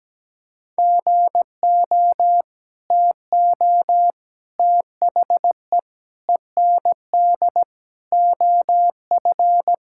Morse code, heard as a keyed tone, about 17 wpm, 700 Hz. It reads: GO TO THE END OF
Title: Sentences from Top 1000 Words - Review Part 5 - 17wpm